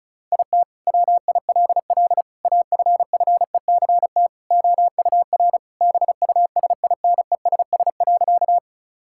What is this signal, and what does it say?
Morse code, heard as a keyed tone, about 36 wpm, 700 Hz. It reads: IT WILL AFFECT OUR BUSINESS.
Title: Sentences from Top 1000 Words - Review Part 6 - 35wpm